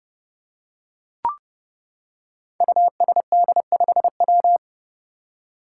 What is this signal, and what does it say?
Morse code, heard as a keyed tone, about 28 wpm, 700 Hz. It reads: USD5W
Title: Five Challenging Letter-Number Combinations 30wpm